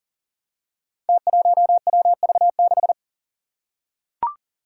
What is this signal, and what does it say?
Morse code, heard as a keyed tone, about 42 wpm, 700 Hz. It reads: T1WV6
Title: Five Challenging Letter-Number Combinations 40wpm